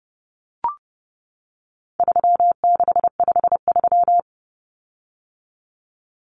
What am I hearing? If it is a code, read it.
Morse code, heard as a keyed tone, about 30 wpm, 700 Hz. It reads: 3653
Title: Common Cut Numbers - 4-Digit 30wpm